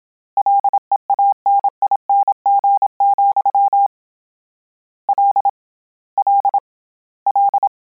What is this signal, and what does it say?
Morse code, heard as a keyed tone, about 27 wpm, 800 Hz. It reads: LEANING, L L L